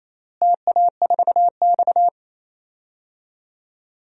Morse code, heard as a keyed tone, about 28 wpm, 700 Hz. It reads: TA4X
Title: DXCC-100 Call Signs - 4 Characters - Repeated 2x - 28wpm